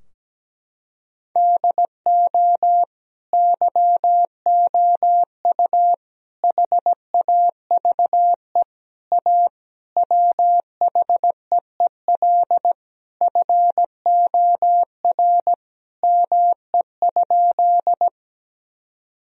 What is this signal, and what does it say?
Morse code, heard as a keyed tone, about 17 wpm, 700 Hz. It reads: DO YOU HAVE A WHEEL FOR ME?
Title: Sentences from Top 1000 Words - Review Part 3 - 17wpm